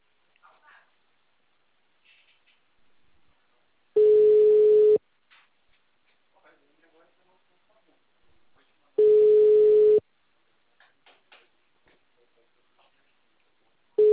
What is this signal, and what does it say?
Ringback tone (Europe, 425 Hz), with speech.